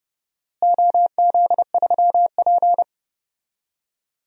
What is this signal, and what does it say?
Morse code, heard as a keyed tone, about 31 wpm, 700 Hz. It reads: OZ3P
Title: DXCC-100 Call Signs - 4 Characters - Repeated 3x - 30wpm